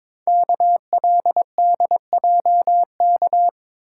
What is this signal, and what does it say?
Morse code, heard as a keyed tone, about 22 wpm, 700 Hz. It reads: KLDJK
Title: Code Groups - 5 Letters - 22wpm